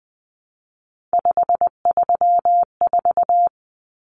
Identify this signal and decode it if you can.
Morse code, heard as a keyed tone, about 20 wpm, 700 Hz. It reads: 534
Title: Common Cut Numbers - 3-Digit 20wpm